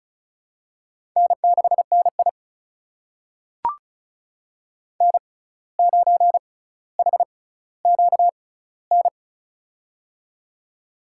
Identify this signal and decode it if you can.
Morse code, heard as a keyed tone, about 36 wpm, spaced as for 17 wpm, 700 Hz. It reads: N6NI N9HQN
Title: USA Call Signs - 6x Character Spacing - 35wpm